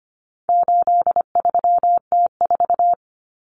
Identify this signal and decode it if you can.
Morse code, heard as a keyed tone, about 25 wpm, 700 Hz. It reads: 83T4
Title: Common Cut Numbers - 4-Digit 25wpm